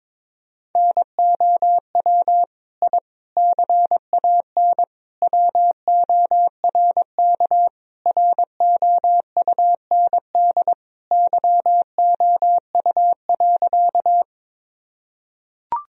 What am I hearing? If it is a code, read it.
Morse code, heard as a keyed tone, about 22 wpm, 700 Hz. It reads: NOW I CAN WORK ROUND YOU.
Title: Sentences from Top 1000 Words - Review Part 1 - 22wpm